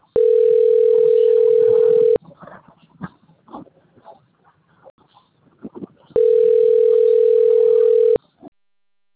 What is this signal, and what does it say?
Ringback tone (North America), 440 + 480 Hz.
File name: rg-1600-unknown- - - .21919